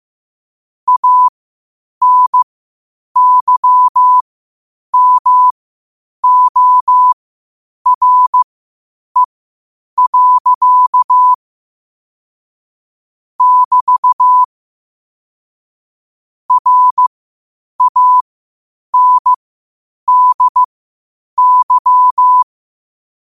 Morse code, heard as a keyed tone, about 15 wpm, spaced as for 8 wpm, 1000 Hz. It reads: ANYMORE. <BT> RANDY
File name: qotdfiles / 8wpm